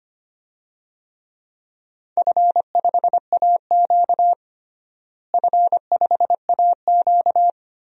Morse code, heard as a keyed tone, about 25 wpm, 700 Hz. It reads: F5AQ F5AQ
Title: DXCC-100 Call Signs - 4 Characters - Repeated 2x - 25wpm